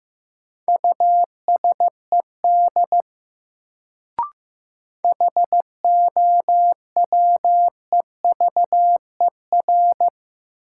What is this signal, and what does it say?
Morse code, heard as a keyed tone, about 15 wpm, 700 Hz. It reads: USED HOWEVER
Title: Sentences from Top 600 Words - New Word Drill - 15wpm